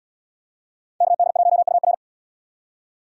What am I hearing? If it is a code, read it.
Morse code, heard as a keyed tone, about 66 wpm, 700 Hz. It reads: KD1UR